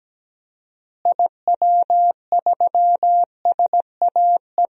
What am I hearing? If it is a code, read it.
Morse code, heard as a keyed tone, about 17 wpm, 700 Hz. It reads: IW3SAE
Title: DXCC-100 Call Signs - 6 Characters - Repeated 3x - 17wpm